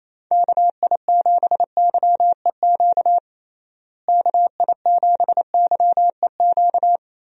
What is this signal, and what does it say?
Morse code, heard as a keyed tone, about 28 wpm, 700 Hz. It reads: KI7YEQ KI7YEQ
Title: USA Call Signs - 6 Characters - Repeated 3x - 28wpm